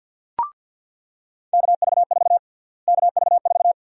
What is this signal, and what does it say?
Morse code, heard as a keyed tone, about 48 wpm, 700 Hz. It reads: KV4 KV4
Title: USA Call Sign Prefixes - Repeated 2x - 50wpm